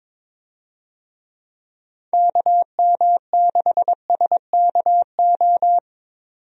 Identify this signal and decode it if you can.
Morse code, heard as a keyed tone, about 22 wpm, 700 Hz. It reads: KM6SKO
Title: USA Call Signs - 6 Characters - 22wpm